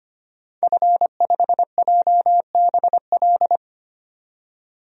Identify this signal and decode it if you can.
Morse code, heard as a keyed tone, about 25 wpm, 700 Hz. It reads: F5JBL
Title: DXCC-100 Call Signs - 5 Characters - 25wpm